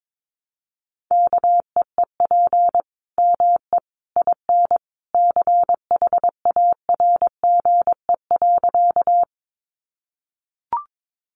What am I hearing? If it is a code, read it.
Morse code, heard as a keyed tone, about 22 wpm, 700 Hz. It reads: KEEP ME IN CHARGE.
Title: Sentences from Top 900 Words 22wpm